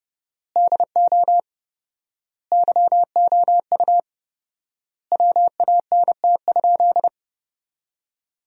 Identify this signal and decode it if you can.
Morse code, heard as a keyed tone, about 29 wpm, 700 Hz. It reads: DO YOU WANT?
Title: 4x Word-Spacing - Sentences from Top 400 Words - 30wpm